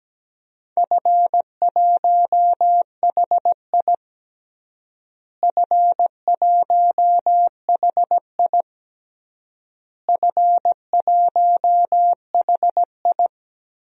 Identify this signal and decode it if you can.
Morse code, heard as a keyed tone, about 17 wpm, 700 Hz. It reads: F1HI F1HI F1HI
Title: DXCC-100 Call Signs - 4 Characters - Repeated 3x - 17wpm